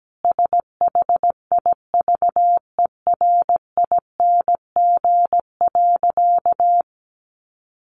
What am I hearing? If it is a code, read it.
Morse code, heard as a keyed tone, about 17 wpm, 700 Hz. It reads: SHIVERING.
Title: A Sound Of Thunder 17wpm